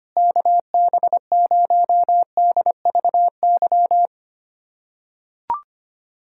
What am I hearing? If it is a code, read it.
Morse code, heard as a keyed tone, about 25 wpm, 700 Hz. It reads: KB0DVY
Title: USA Callsign DE Callsign - 6 Characters - 25wpm